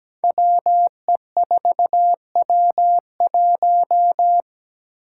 Morse code, heard as a keyed tone, about 17 wpm, 700 Hz. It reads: WE4W1
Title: Five Challenging Letter-Number Combinations 17wpm